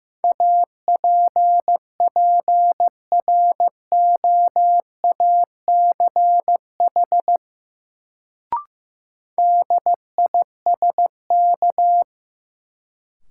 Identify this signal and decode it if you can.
Morse code, heard as a keyed tone, about 15 wpm, 700 Hz. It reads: APPROACH DISK
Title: Sentences from Top 1000 Words - Review Part 6 - New Word Drill - 15wpm